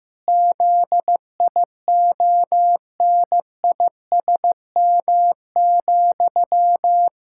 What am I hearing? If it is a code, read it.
Morse code, heard as a keyed tone, about 15 wpm, 700 Hz. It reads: ZIONISM,
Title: Pangram Challenge v2 - 15wpm